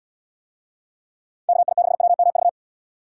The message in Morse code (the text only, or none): KI6WMV